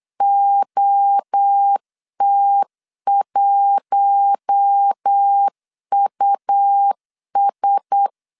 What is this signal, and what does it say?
Morse code, heard as a keyed tone, about 8 wpm, 800 Hz. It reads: OT1US
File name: 8wpm